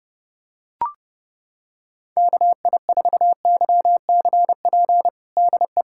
Morse code, heard as a keyed tone, about 30 wpm, 700 Hz. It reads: KI4YCP DE
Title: USA Callsign DE Callsign - 6 Characters - 30wpm